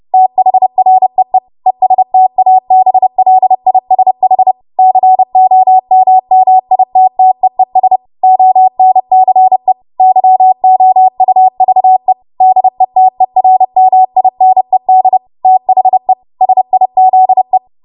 Morse code, heard as a keyed tone, about 30 wpm, 750 Hz. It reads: THREE ESTABLISH COMMITTEES ONCE YOUVE DETERMINED THE SIZE
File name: Morse / 30WPM